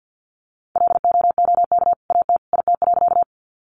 Morse code, heard as a keyed tone, about 50 wpm, 700 Hz. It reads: LOOK AT IT.
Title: Sentences from Top 100 Words 50wpm